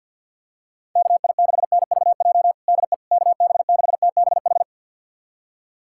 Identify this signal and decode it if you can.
Morse code, heard as a keyed tone, about 49 wpm, 700 Hz. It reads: KI6NVJ DE KB6TBH
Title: USA Callsign DE Callsign - 6 Characters - 50wpm